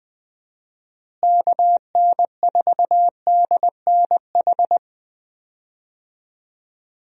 Morse code, heard as a keyed tone, about 20 wpm, 700 Hz. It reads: KN4DNH
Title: USA Call Signs - 6 Characters - Repeated 3x - 20wpm